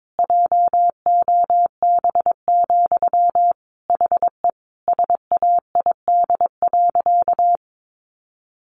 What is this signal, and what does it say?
Morse code, heard as a keyed tone, about 22 wpm, 700 Hz. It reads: JOB, HE SAID.
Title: Sentences from Top 1500 Words 22wpm